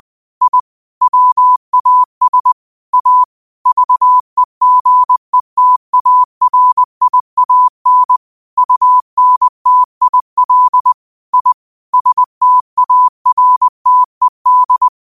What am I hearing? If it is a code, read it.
Morse code, heard as a keyed tone, about 20 wpm, 1000 Hz. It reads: I WAS A VEGETARIAN UNTIL I STARTED
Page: Daily quotes in Morse Code at 20 Words per Minute. Use this to improve your code copying skill.